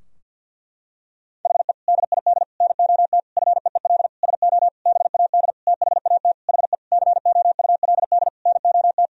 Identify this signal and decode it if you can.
Morse code, heard as a keyed tone, about 50 wpm, 700 Hz. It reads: HE DID NOT FEEL SO BAD THAT HE COULD NOT
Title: Sentences from Top 1000 Words - Review Part 4 - 50wpm